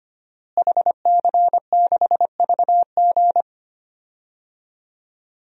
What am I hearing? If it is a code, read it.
Morse code, heard as a keyed tone, about 25 wpm, 700 Hz. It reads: HC6VG
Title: DXCC-100 Call Signs - 5 Characters - Repeated 3x - 25wpm